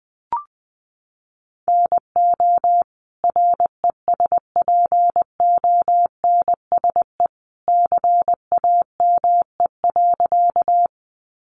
Morse code, heard as a keyed tone, about 20 wpm, 700 Hz. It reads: NO RESPONSE CAME.
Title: Sentences from Top 1400 Words 20wpm